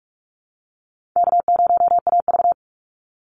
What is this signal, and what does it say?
Morse code, heard as a keyed tone, about 44 wpm, 700 Hz. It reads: K0AV
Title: Call Signs Repeated 2x - 45wpm